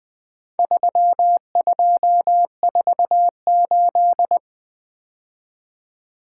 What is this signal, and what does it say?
Morse code, heard as a keyed tone, about 22 wpm, 700 Hz. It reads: 3248
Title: Speed-Racing - Four Letter-Number Combinations - 20wpm